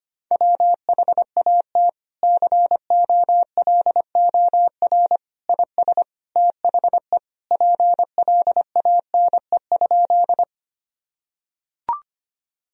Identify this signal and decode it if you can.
Morse code, heard as a keyed tone, about 25 wpm, 700 Hz. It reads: WHAT COLOR IS THE PLANE?
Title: Sentences from Top 1000 Words - Review Part 3 - 25wpm